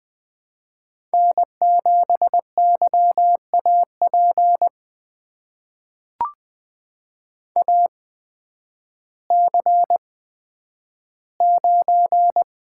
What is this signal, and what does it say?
Morse code, heard as a keyed tone, about 20 wpm, spaced as for 11 wpm, 700 Hz. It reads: N7YAP AC9
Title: USA Call Signs - 5 Characters - 8x Character Spacing - 20wpm